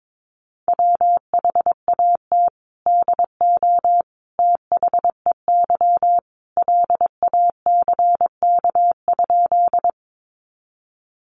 Morse code, heard as a keyed tone, about 22 wpm, 700 Hz. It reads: WHAT DO THEY LACK?
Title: Sentences from Top 1100 Words 22wpm